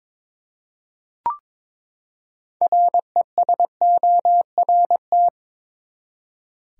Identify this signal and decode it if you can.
Morse code, heard as a keyed tone, about 22 wpm, 700 Hz. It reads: RESORT
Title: Six Letter Words - 22wpm